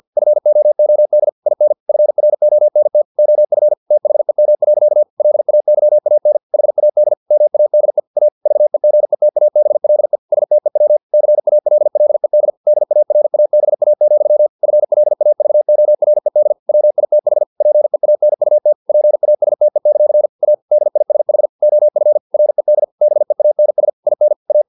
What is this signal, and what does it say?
Morse code, heard as a keyed tone, about 50 wpm, 600 Hz. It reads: FOOD IN FRONT OF THEM. LAYAN HAD MADE A VEGETABLE STEW CALLED DARABA, FLAVORED WITH PEANUT PASTE; A DISH OF RED BEANS IN A
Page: Never 49wpm